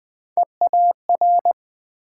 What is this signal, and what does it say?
Morse code, heard as a keyed tone, about 23 wpm, 700 Hz. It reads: EAR